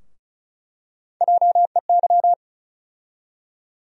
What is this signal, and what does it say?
Morse code, heard as a keyed tone, about 36 wpm, 700 Hz. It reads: JEY
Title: USA Call Sign Postfixes - 3 Letters 35wpm